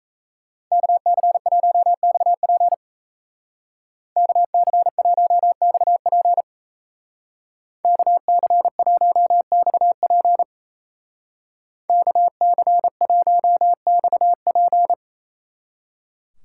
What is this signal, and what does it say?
Morse code, heard as a keyed tone, about 35 wpm, 700 Hz. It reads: KC1XP KC1XP KC1XP KC1XP
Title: Speed Racing - USA Call Signs 28wpm